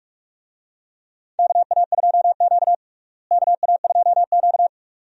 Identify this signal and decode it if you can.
Morse code, heard as a keyed tone, about 46 wpm, 700 Hz. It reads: KA2Q KA2Q
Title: USA Call Signs - 4 Characters - Repeated 2x - 45wpm